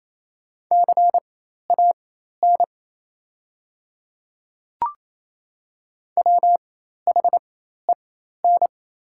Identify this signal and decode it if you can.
Morse code, heard as a keyed tone, about 28 wpm, spaced as for 13 wpm, 700 Hz. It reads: CAN WHEN
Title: 4x Character Spacing - Top 300 Words - 28wpm